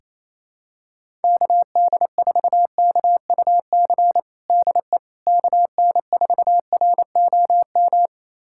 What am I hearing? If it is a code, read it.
Morse code, heard as a keyed tone, about 28 wpm, 700 Hz. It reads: KD4KUC DE KN4ROM
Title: USA Callsign DE Callsign 28wpm